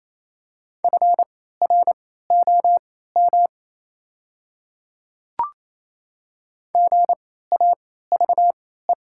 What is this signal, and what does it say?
Morse code, heard as a keyed tone, about 28 wpm, spaced as for 16 wpm, 690 Hz. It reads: FROM GAVE
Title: 3x Character Spacing - Top 500 Words - 28wpm